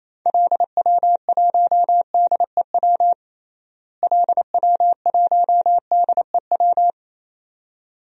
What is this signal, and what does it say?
Morse code, heard as a keyed tone, about 28 wpm, 700 Hz. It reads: LW1DEW LW1DEW
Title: DXCC-100 Call Signs - 6 Characters - Repeated 2x - 28wpm